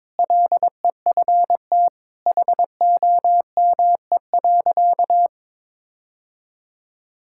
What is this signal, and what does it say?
Morse code, heard as a keyed tone, about 22 wpm, 700 Hz. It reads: LEFT HOME.
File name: practice / 22wpm / Sentences with 4-Word Phrases - 22wpm